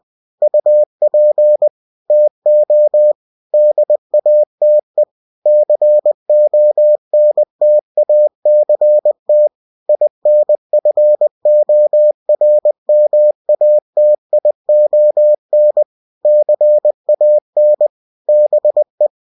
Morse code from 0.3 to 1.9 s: UP